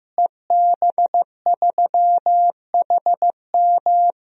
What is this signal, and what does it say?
Morse code, heard as a keyed tone, about 15 wpm, 700 Hz. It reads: EB3HM
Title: DXCC-100 Call Signs - 5 Characters - Repeated 2x - 15wpm